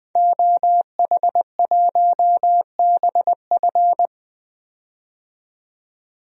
Morse code, heard as a keyed tone, about 20 wpm, 700 Hz. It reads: OH1BF
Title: DXCC-100 Call Signs - 5 Characters - 20wpm